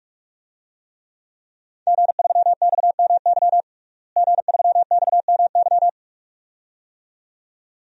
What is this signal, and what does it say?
Morse code, heard as a keyed tone, about 41 wpm, 700 Hz. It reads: G3XMY G3XMY
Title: DXCC-100 Call Signs - 5 Characters - Repeated 2x - 45wpm